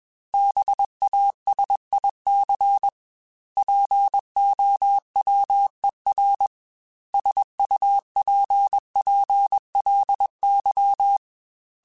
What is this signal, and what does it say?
Morse code, heard as a keyed tone, about 21 wpm, 800 Hz. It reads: BASIC POWER SUPPLY
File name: Technical Subjects / Technical Subjects-mp3s / Sw Pwr Sup @20